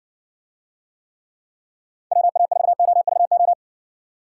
Morse code, heard as a keyed tone, about 60 wpm, 700 Hz. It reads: WA4OVO